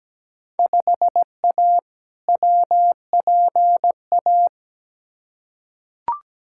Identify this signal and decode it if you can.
Morse code, heard as a keyed tone, about 17 wpm, 700 Hz. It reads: 5A WPA
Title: Field Day Exchanges Encore - 17wpm